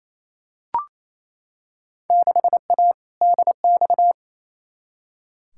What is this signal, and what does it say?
Morse code, heard as a keyed tone, about 28 wpm, 700 Hz. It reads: 6A DX
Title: Field Day Exchanges Encore - 28wpm